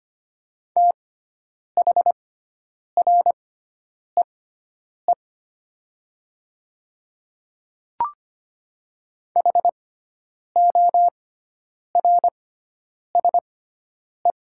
Morse code, heard as a keyed tone, about 25 wpm, spaced as for 9 wpm, 700 Hz. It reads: THREE HORSE
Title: 6x Character Spacing - Top 300 Words - 25wpm